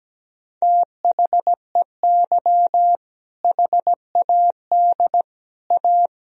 Morse code, heard as a keyed tone, about 17 wpm, 700 Hz. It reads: THEY HAD A
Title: Sentences from Top 1000 Words - Review Part 2 - 17wpm